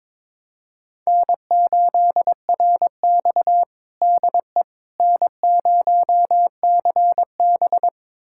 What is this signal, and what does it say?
Morse code, heard as a keyed tone, about 22 wpm, 700 Hz. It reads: N8RX DE N0CB
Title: USA Callsign DE Callsign - 4 Characters - 22wpm